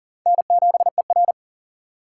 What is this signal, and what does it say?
Morse code, heard as a keyed tone, about 40 wpm, 700 Hz. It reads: N7ER
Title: USA Call Signs - 4 Characters - 2x Character Spacing - 40wpm